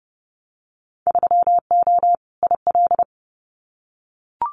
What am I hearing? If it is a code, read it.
Morse code, heard as a keyed tone, about 30 wpm, 700 Hz. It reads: 3O IL
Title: Winter Field Day Exchanges 30wpm